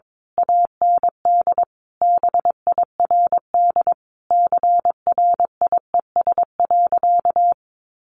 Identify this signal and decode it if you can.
Morse code, heard as a keyed tone, about 22 wpm, 700 Hz. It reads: AND BIRD CRIES.
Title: A Sound Of Thunder 22wpm